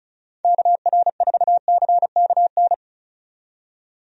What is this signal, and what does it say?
Morse code, heard as a keyed tone, about 34 wpm, 700 Hz. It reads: KR4CKN
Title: USA Call Signs - 6 Characters - 35wpm